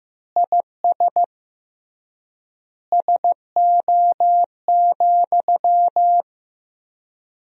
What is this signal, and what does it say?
Morse code, heard as a keyed tone, about 15 wpm, 700 Hz. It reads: IS SO,
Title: 3x Word-Spacing - Sentences from Top 200 Words - 15wpm